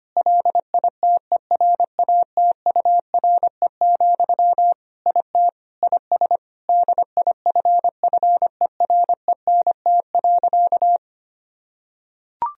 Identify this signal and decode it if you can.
Morse code, heard as a keyed tone, about 25 wpm, 700 Hz. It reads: LITERATURE, IT IS DIFFERENT.
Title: Sentences from Top 1000 Words - Review Part 3 - 25wpm